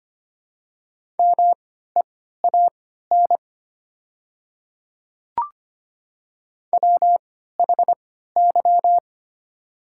Morse code, heard as a keyed tone, about 25 wpm, spaced as for 14 wpm, 700 Hz. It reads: MEAN WHY
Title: 3x Character Spacing - Top 200 Words - 25wpm